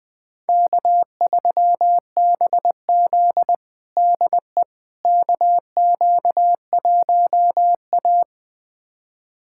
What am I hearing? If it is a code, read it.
Morse code, heard as a keyed tone, about 21 wpm, 700 Hz. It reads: K3BZ DE KQ1A
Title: USA Callsign DE Callsign - 4 Characters - 20wpm